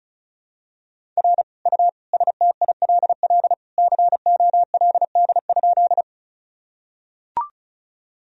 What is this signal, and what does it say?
Morse code, heard as a keyed tone, about 34 wpm, 700 Hz. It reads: R U STILL COLD?
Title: Sentences from Top 400 Words and Abbreviations 35wpm